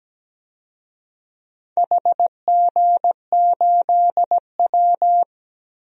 Morse code, heard as a keyed tone, about 17 wpm, 700 Hz. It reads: HG8W
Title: DXCC-100 Call Signs - 4 Characters - Repeated 3x - 17wpm